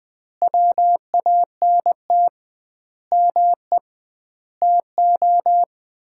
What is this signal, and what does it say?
Morse code, heard as a keyed tone, about 20 wpm, 700 Hz. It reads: WANT ME TO
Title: 2x Word-Spacing - Sets of 5 Words - 20wpm